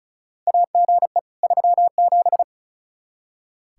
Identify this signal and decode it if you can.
Morse code, heard as a keyed tone, about 36 wpm, 700 Hz. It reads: AGE 37
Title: K1USN SST QSO Practice 35wpm